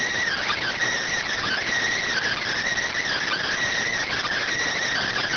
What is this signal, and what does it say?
Emergency Alert System (SAME) tones, heard as harsh buzzing bursts. It is heard from 0.2 to 4.2 s.